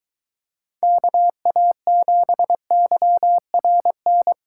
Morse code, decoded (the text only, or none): KA7YRN